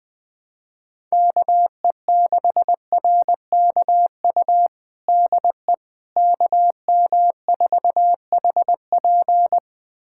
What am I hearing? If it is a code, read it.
Morse code, heard as a keyed tone, about 20 wpm, 700 Hz. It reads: KE6RKU DE KM4HP
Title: USA Callsign DE Callsign 20wpm